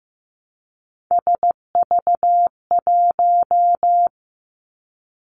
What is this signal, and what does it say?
Morse code, heard as a keyed tone, about 15 wpm, 700 Hz. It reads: SV1
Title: Callsign Prefixes for Top 50 DXCC - 15wpm